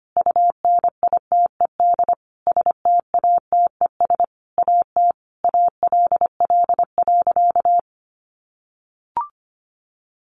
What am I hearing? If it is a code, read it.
Morse code, heard as a keyed tone, about 25 wpm, 700 Hz. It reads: UNITED STATES AT ALL.
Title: A Sound Of Thunder 25wpm